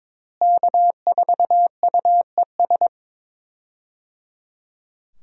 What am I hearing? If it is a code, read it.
Morse code, heard as a keyed tone, about 22 wpm, 700 Hz. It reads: K4UES
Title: USA Call Signs - 5 Characters - Repeated 3x - 22wpm